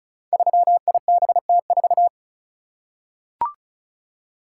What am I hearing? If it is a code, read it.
Morse code, heard as a keyed tone, about 36 wpm, 700 Hz. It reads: 3IBT4
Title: Code Groups - 5 Letter:Numbers - 35wpm